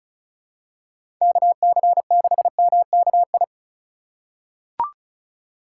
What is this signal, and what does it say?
Morse code, heard as a keyed tone, about 36 wpm, 700 Hz. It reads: KC6MKI
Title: USA Call Signs - 6 Characters - 4x Character Spacing - 35wpm